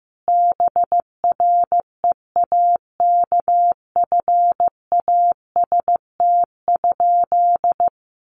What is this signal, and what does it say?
Morse code, heard as a keyed tone, about 15 wpm, 700 Hz. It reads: BREAKFAST?
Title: Sentences from Top 1500 Words 15wpm